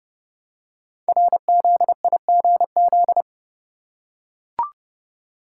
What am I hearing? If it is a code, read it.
Morse code, heard as a keyed tone, about 30 wpm, 700 Hz. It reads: RZIGZ
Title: Code Groups - 5 Letters - 30wpm